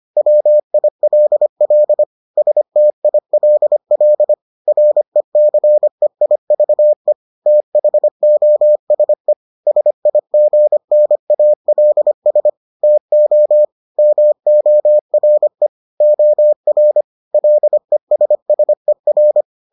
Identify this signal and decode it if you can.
Morse code, heard as a keyed tone, about 25 wpm, 600 Hz. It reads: WILL STILL RECEIVE THOSE SIGNALS TO MORE OR LESSER